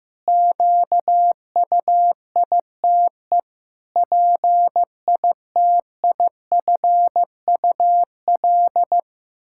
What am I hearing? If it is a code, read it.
Morse code, heard as a keyed tone, about 15 wpm, 700 Hz. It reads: QUITE PITIFUL
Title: Pangram Challenge v2 - 15wpm